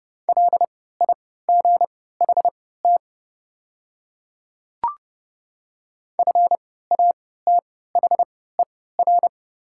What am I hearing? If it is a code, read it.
Morse code, heard as a keyed tone, about 29 wpm, spaced as for 17 wpm, 700 Hz. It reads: LIGHT FATHER
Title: 3x Character Spacing - Top 200 Words - 30wpm